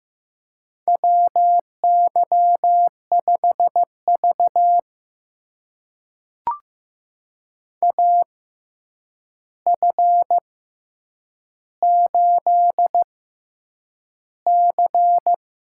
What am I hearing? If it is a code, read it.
Morse code, heard as a keyed tone, about 15 wpm, spaced as for 8 wpm, 700 Hz. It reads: WY5V AF8C
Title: USA Call Signs - 4 Characters - 6x Character Spacing - 15wpm